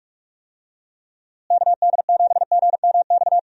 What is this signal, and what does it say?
Morse code, heard as a keyed tone, about 46 wpm, 700 Hz. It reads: KD7GMX
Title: USA Call Signs - 6 Characters - Repeated 3x - 45wpm